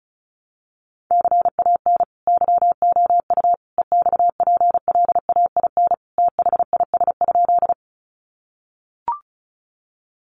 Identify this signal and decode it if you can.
Morse code, heard as a keyed tone, about 34 wpm, 700 Hz. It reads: CAN YOU EXPLAIN THIS?
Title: Sentences from Top 1100 Words 35wpm